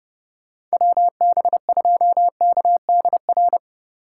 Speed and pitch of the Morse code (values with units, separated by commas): 38 wpm, 700 Hz